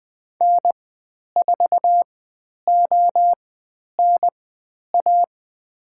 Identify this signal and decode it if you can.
Morse code, heard as a keyed tone, about 20 wpm, spaced as for 10 wpm, 700 Hz. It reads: N4ONA
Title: 109 Lesson 18g - Letter m - Call Signs 20wpm